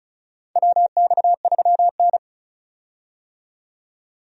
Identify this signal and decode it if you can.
Morse code, heard as a keyed tone, about 36 wpm, 700 Hz. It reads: WX3N
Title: USA Call Signs - 4 Characters - 35wpm